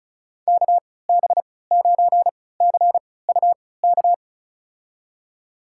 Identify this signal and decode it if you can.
Morse code, heard as a keyed tone, about 36 wpm, spaced as for 20 wpm, 700 Hz. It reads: KB9CUK
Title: USA Call Signs - 3x Character Spacing - 35wpm